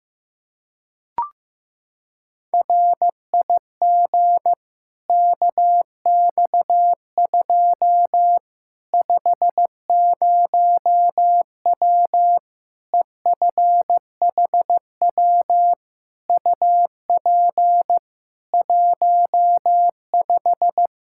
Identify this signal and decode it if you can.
Morse code, heard as a keyed tone, about 15 wpm, 700 Hz. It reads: RIG KX2 50W EFHW UP 15
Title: K1USN SST QSO Practice 15wpm